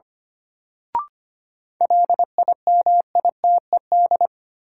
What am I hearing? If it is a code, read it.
Morse code, heard as a keyed tone, about 25 wpm, 700 Hz. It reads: LIMITED
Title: Sentences from Top 900 Words - New Word Drill - 25wpm